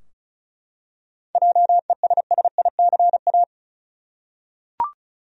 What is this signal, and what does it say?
Morse code, heard as a keyed tone, about 36 wpm, 700 Hz. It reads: JESSICA
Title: USA Common Names - 35wpm